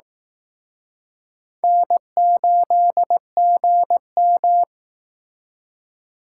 Morse code, heard as a keyed tone, about 18 wpm, 700 Hz. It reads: N8GM
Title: Speed Racing - USA 1x2 Call Signs 15wpm